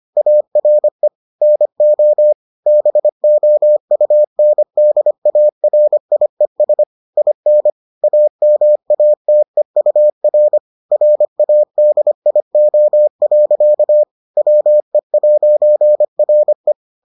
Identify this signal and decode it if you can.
Morse code, heard as a keyed tone, about 25 wpm, 600 Hz. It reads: ARE NO BOUNDARIES IN AMATEUR RADIO. WE'RE